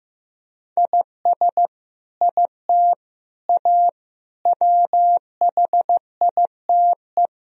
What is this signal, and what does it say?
Morse code, heard as a keyed tone, about 15 wpm, 700 Hz. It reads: IS IT A WHITE
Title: Sentences from Top 1000 Words - Review Part 1 - 15wpm